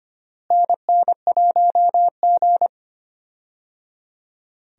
Morse code, heard as a keyed tone, about 25 wpm, 700 Hz. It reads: NN1G
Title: USA Call Signs - 4 Characters - Repeated 3x - 25wpm